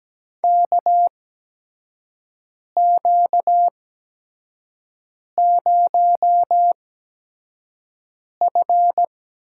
Morse code, heard as a keyed tone, about 17 wpm, spaced as for 5 wpm, 700 Hz. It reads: KQ0F
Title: USA Call Signs - 4 Characters - 8x Character Spacing - 17wpm